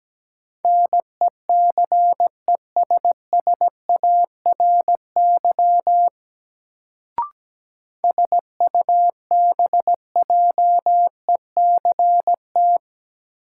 Morse code, heard as a keyed tone, about 17 wpm, 700 Hz. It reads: NECESSARY SUBJECT
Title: Sentences from Top 800 Words - New Word Drill - 17wpm